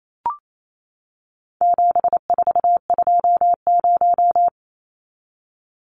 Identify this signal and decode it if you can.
Morse code, heard as a keyed tone, about 28 wpm, 700 Hz. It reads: 7420
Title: Numbers - 4-Digit 28wpm